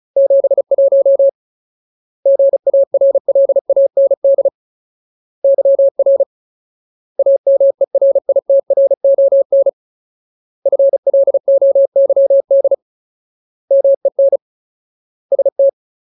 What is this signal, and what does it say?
Morse code, heard as a keyed tone, about 34 wpm, 550 Hz. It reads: 71 GARLAND YR AMERITRON FLOYD MEN ST
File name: Ultimate Word List / 35WPM Ultimate